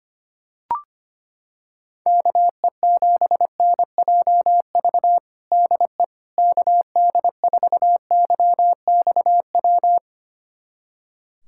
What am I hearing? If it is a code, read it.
Morse code, heard as a keyed tone, about 25 wpm, 700 Hz. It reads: KE7NJV DE KD4YXW
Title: USA Callsign DE Callsign 25wpm